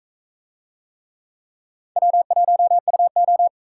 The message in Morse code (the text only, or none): W1UO